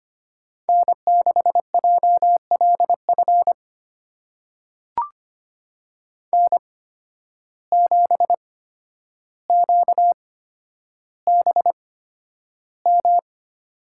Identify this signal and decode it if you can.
Morse code, heard as a keyed tone, about 25 wpm, spaced as for 11 wpm, 700 Hz. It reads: N6JLF N7QBM
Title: USA Call Signs - 5 Characters - 8x Character Spacing - 25wpm